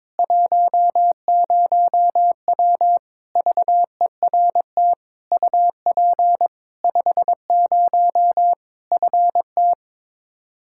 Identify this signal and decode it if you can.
Morse code, heard as a keyed tone, about 22 wpm, 700 Hz. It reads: 10W VERT UP 50 FT
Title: Practice QSO - Exchange 2 - Info - 22wpm